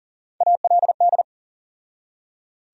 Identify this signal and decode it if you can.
Morse code, heard as a keyed tone, about 40 wpm, 700 Hz. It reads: ALD
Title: USA Call Sign Postfixes - 3 Letters 40wpm